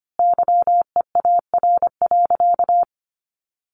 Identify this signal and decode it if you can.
Morse code, heard as a keyed tone, about 25 wpm, 700 Hz. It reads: YEAR.
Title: Sentences from Top 1400 Words 25wpm